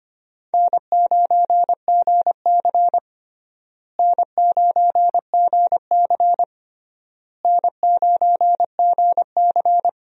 Morse code, heard as a keyed tone, about 25 wpm, 700 Hz. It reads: N9GC N9GC N9GC
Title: USA Call Signs - 4 Characters - Repeated 3x - 25wpm